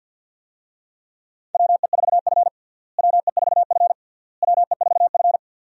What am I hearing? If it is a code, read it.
Morse code, heard as a keyed tone, about 46 wpm, 700 Hz. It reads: WE4F WE4F WE4F
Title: USA Call Signs - 4 Characters - Repeated 3x - 50wpm